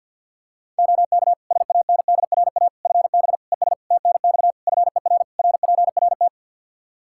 Morse code, heard as a keyed tone, about 50 wpm, 700 Hz. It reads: OK SANDRA FB ES TNX FER RPRT
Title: Practice QSO - Exchange 2 - Info - 50wpm